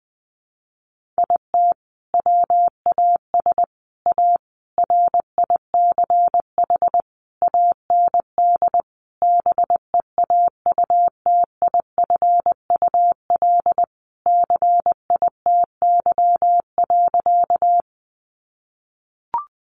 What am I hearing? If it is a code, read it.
Morse code, heard as a keyed tone, about 20 wpm, 700 Hz. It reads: IT WAS A RICH AND BEAUTIFUL CITY.
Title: Sentences from Top 1100 Words 20wpm